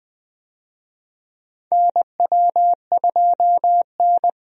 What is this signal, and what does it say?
Morse code, heard as a keyed tone, about 20 wpm, 700 Hz. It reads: NW2N
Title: USA Call Signs - 4 Characters - 20wpm